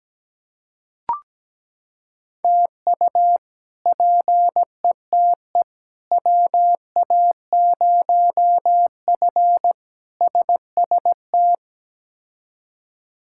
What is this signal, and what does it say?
Morse code, heard as a keyed tone, about 17 wpm, 700 Hz. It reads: TU PETE WA0F SST
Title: K1USN SST QSO Practice - 17wpm